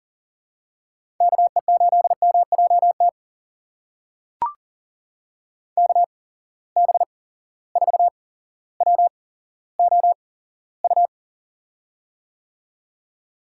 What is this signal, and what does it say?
Morse code, heard as a keyed tone, about 40 wpm, spaced as for 17 wpm, 700 Hz. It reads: KE8MJT KB4WOU
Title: USA Call Signs - 6 Characters - 8x Character Spacing - 40wpm